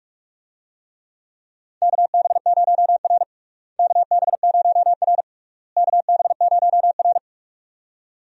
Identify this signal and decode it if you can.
Morse code, heard as a keyed tone, about 47 wpm, 700 Hz. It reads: KB0R KB0R KB0R
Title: USA Call Signs - 4 Characters - Repeated 3x - 45wpm